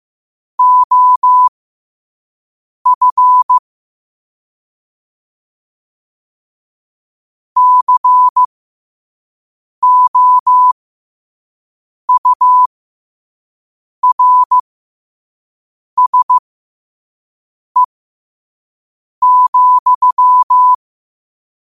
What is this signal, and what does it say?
Morse code, heard as a keyed tone, about 15 wpm, spaced as for 5 wpm, 1000 Hz. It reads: OF COURSE,
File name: 5wpm